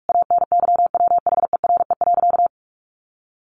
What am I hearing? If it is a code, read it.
Morse code, heard as a keyed tone, about 44 wpm, 700 Hz. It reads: ANYWHERE.
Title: Sentences from Top 1400 Words 45wpm